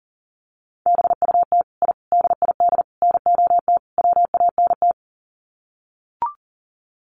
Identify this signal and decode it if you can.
Morse code, heard as a keyed tone, about 40 wpm, 700 Hz. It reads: BUT I DID NOT WANT
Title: Sets of 5 Words Encore 40wpm